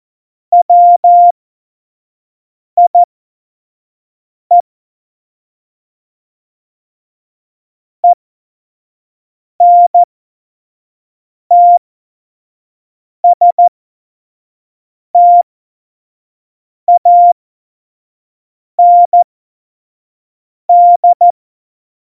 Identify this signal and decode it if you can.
Morse code, heard as a keyed tone, about 14 wpm, spaced as for 5 wpm, 700 Hz. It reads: WIE ENTSTAND
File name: Morsekurs / Witze / Witz 07 Juli 2020, 25 F70